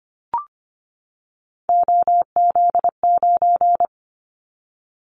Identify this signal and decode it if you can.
Morse code, heard as a keyed tone, about 25 wpm, 700 Hz. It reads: OZ9
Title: Callsign Prefixes for Top 50 DXCC - 25wpm